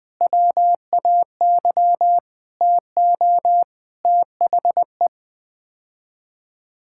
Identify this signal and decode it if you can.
Morse code, heard as a keyed tone, about 21 wpm, 700 Hz. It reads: WAY TO THE
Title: Sets of 9 Words - 20wpm